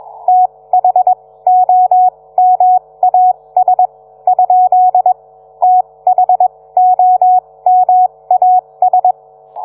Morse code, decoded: THOMAS ? THOMAS